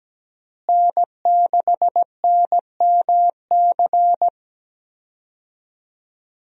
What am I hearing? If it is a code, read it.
Morse code, heard as a keyed tone, about 17 wpm, 700 Hz. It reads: N6NMC
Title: USA Call Signs - 5 Characters - Repeated 2x - 17wpm